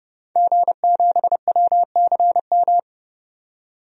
Morse code, heard as a keyed tone, about 30 wpm, 700 Hz. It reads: G7WCM
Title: DXCC-100 Call Signs - 5 Characters - 30wpm